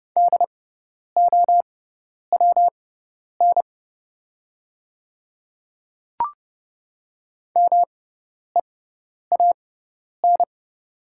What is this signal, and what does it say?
Morse code, heard as a keyed tone, about 30 wpm, spaced as for 10 wpm, 700 Hz. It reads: DOWN MEAN
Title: 6x Character Spacing - Top 200 Words - 30wpm